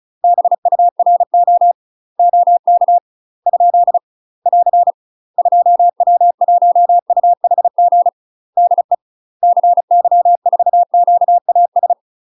Morse code, heard as a keyed tone, about 36 wpm, 700 Hz. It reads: BURO OK ? <AR> 2W1UHG DE CY4QAS
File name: UK MASTER QSOs TRAINING/35WPM